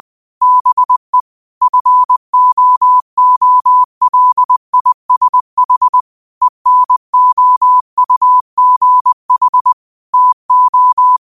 Morse code, heard as a keyed tone, about 20 wpm, 1000 Hz. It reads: BE FOOLISH ENOUGH TO
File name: qotdfiles / 20wpm